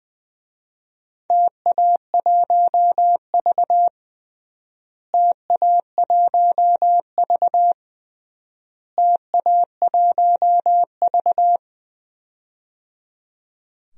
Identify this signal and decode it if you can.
Morse code, heard as a keyed tone, about 20 wpm, 700 Hz. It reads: TA1V TA1V TA1V
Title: DXCC-100 Call Signs - 4 Characters - Repeated 3x - 20wpm